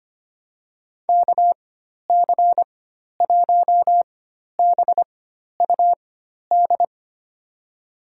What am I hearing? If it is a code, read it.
Morse code, heard as a keyed tone, about 25 wpm, spaced as for 12 wpm, 700 Hz. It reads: KC1BUD
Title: USA Call Signs - 4x Character Spacing - 25wpm